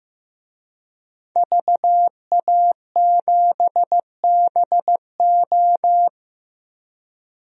Morse code, heard as a keyed tone, about 15 wpm, 700 Hz. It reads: VA7BO
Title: DXCC-100 Call Signs - 5 Characters - 15wpm